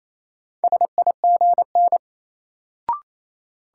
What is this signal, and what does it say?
Morse code, heard as a keyed tone, about 28 wpm, 700 Hz. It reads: SIGN
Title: Sentences from Top 1000 Words - New Word Drill - 28wpm